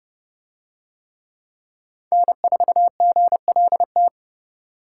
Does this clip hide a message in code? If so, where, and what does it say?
Yes: Morse code from 2.1 to 4.1 s, heard as a keyed tone, about 29 wpm, 700 Hz. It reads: N4GLT